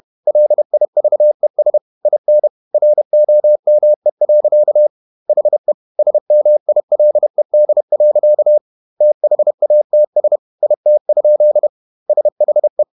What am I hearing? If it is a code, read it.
Morse code, heard as a keyed tone, about 31 wpm, 600 Hz. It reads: LIVES IN ROME. HE SMILED. THATS IT? SHE
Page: Never 31wpm | CW med Gnister